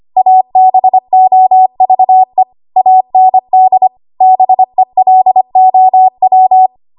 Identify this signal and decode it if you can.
Morse code, heard as a keyed tone, about 25 wpm, 750 Hz. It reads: ABOVE AND BELOW
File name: Archive / 25 WPM / 25WPM